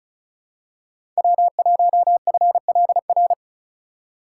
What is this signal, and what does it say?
Morse code, heard as a keyed tone, about 34 wpm, 700 Hz. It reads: W1FLR
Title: USA Call Signs - 2x Character Spacing - 35wpm